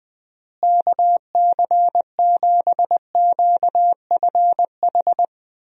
Morse code, heard as a keyed tone, about 20 wpm, 700 Hz. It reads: KC7QFH
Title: USA Call Signs - 6 Characters - Repeated 3x - 20wpm